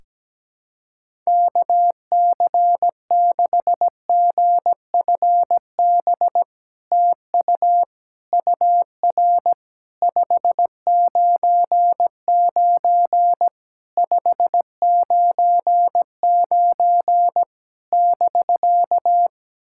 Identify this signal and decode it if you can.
Morse code, heard as a keyed tone, about 17 wpm, 700 Hz. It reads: KC6GFB TU UR 599 599 <BK>
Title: SOTA QSO Practice 17wpm